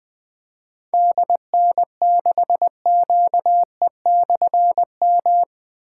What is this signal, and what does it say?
Morse code, heard as a keyed tone, about 20 wpm, 700 Hz. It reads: DN6QE/M
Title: Difficult Call Signs 20wpm